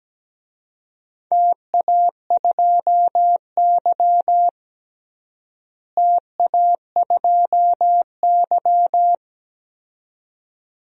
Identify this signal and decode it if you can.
Morse code, heard as a keyed tone, about 17 wpm, 700 Hz. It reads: TA2Y TA2Y
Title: DXCC-100 Call Signs - 4 Characters - Repeated 2x - 17wpm